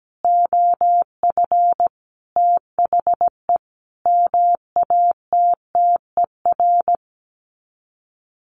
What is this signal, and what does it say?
Morse code, heard as a keyed tone, about 17 wpm, 700 Hz. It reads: OF THE MATTER
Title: Sets of 5 Words 17wpm